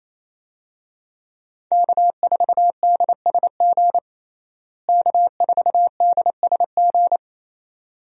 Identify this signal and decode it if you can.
Morse code, heard as a keyed tone, about 28 wpm, 700 Hz. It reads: K4DSG K4DSG
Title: USA Call Signs - 5 Characters - Repeated 2x - 28wpm